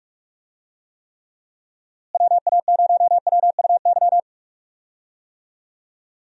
Morse code, heard as a keyed tone, about 46 wpm, 700 Hz. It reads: WA0WUY
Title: USA Call Signs - 6 Characters - Repeated 3x - 45wpm